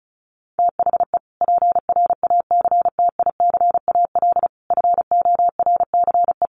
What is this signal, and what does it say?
Morse code, heard as a keyed tone, about 34 wpm, 700 Hz. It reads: THE PRACTICAL FORCE
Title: Sentences from Top 1400 Words 35wpm